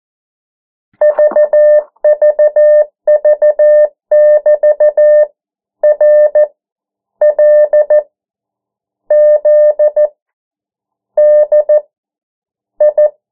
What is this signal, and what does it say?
Morse code, heard as a keyed tone, about 14 wpm, spaced as for 9 wpm, 600 Hz. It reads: VVV<BT>RLZDI
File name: morse mp3 / 09 urok 8